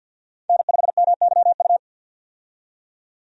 Morse code, heard as a keyed tone, about 50 wpm, 700 Hz. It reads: N5MYU
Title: USA Call Signs - 5 Characters - 50wpm